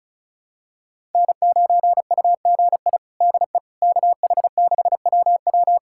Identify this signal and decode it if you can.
Morse code, heard as a keyed tone, about 36 wpm, 700 Hz. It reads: N9UGI DE KH6WW
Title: USA Callsign DE Callsign - 5 Characters - 35wpm